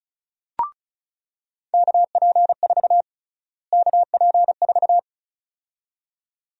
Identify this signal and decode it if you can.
Morse code, heard as a keyed tone, about 36 wpm, 700 Hz. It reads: KP4 KP4
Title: USA Call Sign Prefixes - Repeated 2x - 35wpm